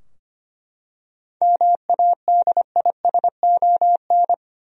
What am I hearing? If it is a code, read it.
Morse code, heard as a keyed tone, about 25 wpm, 700 Hz. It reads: MADISON
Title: USA Common Names - 25wpm